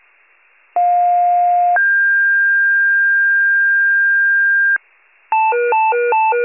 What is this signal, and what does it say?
Two-tone page, heard as a one-second tone, then a three-second tone, about 710 then 1600 Hz, with noise. It is heard from 0.8 to 4.8 s.
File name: wcalert